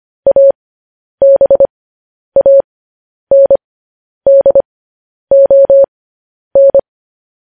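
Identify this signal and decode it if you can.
Morse code, heard as a keyed tone, about 25 wpm, spaced as for 10 wpm, 550 Hz. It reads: ABANDON
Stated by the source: Morse code news headlines sent at 10 wpm.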